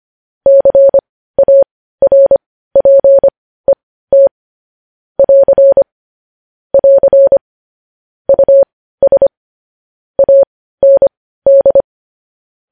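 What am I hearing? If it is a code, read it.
Morse code, heard as a keyed tone, about 25 wpm, spaced as for 15 wpm, 550 Hz. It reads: CARPET <AR> <AR> US AND
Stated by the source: Morse news headlines practice at 15.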